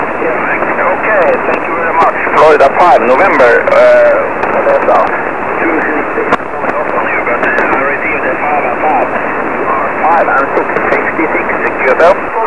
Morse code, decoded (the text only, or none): none